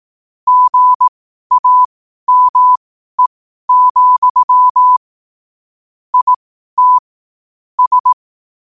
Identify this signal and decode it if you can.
Morse code, heard as a keyed tone, about 18 wpm, spaced as for 12 wpm, 1000 Hz. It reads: GAME, IT S
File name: qotdfiles / 12wpm